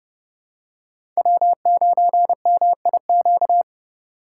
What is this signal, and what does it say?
Morse code, heard as a keyed tone, about 30 wpm, 700 Hz. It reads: W9MIQ
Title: USA Call Signs - 5 Characters - 3x Character Spacing - 30wpm